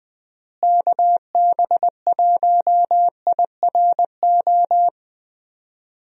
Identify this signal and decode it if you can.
Morse code, heard as a keyed tone, about 21 wpm, 700 Hz. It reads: KB1IRO
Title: USA Call Signs - 6 Characters - Repeated 3x - 20wpm